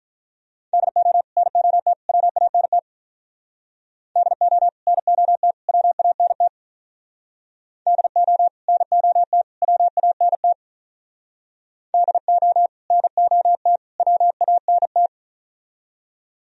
Morse code, heard as a keyed tone, about 45 wpm, 700 Hz. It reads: DO NOT WANT DO NOT WANT DO NOT WANT DO NOT WANT